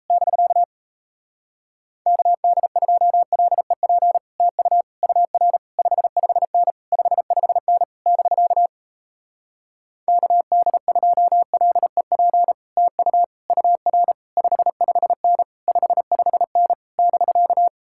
Morse code, heard as a keyed tone, about 37 wpm, 700 Hz. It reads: <BK> KD2LEP TU UR 55N 55N <BK> KD2LEP TU UR 55N 55N <BK>